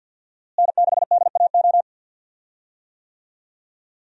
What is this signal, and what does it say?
Morse code, heard as a keyed tone, about 51 wpm, 700 Hz. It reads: N6DAO
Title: USA Call Signs - 5 Characters - Repeated 2x - 50wpm